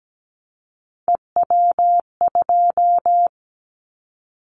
Morse code, heard as a keyed tone, about 17 wpm, 700 Hz. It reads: EW2
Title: Callsign Prefixes for Top 100 DXCC - 17wpm